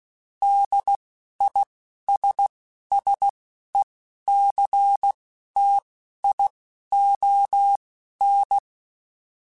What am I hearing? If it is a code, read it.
Morse code, heard as a keyed tone, about 16 wpm, spaced as for 11 wpm, 800 Hz. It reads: DISSECTION
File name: General / General-mp3s / ISS @10